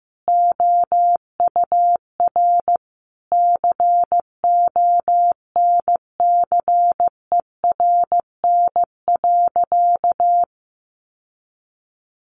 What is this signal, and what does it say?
Morse code, heard as a keyed tone, about 15 wpm, 700 Hz. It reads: OUR CONCERN.
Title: Sentences from Top 1400 Words 15wpm